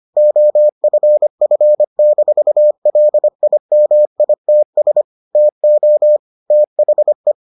Morse code, heard as a keyed tone, about 25 wpm, 600 Hz. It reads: OFF-LIMITS TO THE